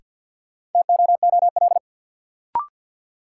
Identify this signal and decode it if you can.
Morse code, heard as a keyed tone, about 51 wpm, 700 Hz. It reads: TOOL
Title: Sentences from Top 1000 Words - Review Part 6 - New Word Drill - 50wpm